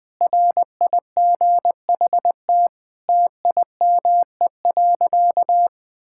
Morse code, heard as a keyed tone, about 20 wpm, 700 Hz. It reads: RIGHT TIME.
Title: Sentences with 7-Word Phrases - 20wpm